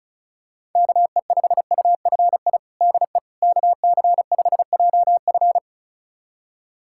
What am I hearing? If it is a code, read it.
Morse code, heard as a keyed tone, about 36 wpm, 700 Hz. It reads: KE5UFI DE KC5JF
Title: USA Callsign DE Callsign 35wpm